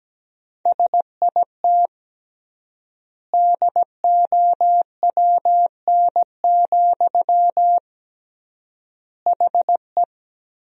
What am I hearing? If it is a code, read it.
Morse code, heard as a keyed tone, about 17 wpm, 700 Hz. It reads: SIT DOWN, HE
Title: 3x Word-Spacing - Sentences from Top 500 Words - 17wpm